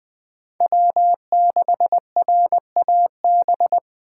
Morse code, heard as a keyed tone, about 20 wpm, 700 Hz. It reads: W6RAB
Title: Speed Racing - USA Call Signs 17wpm